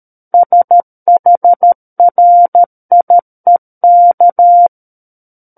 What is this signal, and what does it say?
Morse code, heard as a keyed tone, about 13 wpm, 700 Hz. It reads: SHRIEK
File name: Book One / 04-Chapter 4 - 13 WPM